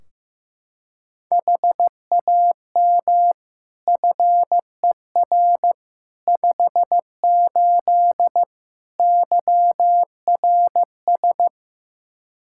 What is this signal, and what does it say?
Morse code, heard as a keyed tone, about 15 wpm, 700 Hz. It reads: HAM FER 58 YRS
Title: Practice QSO - Exchange 3 - More Info - 15wpm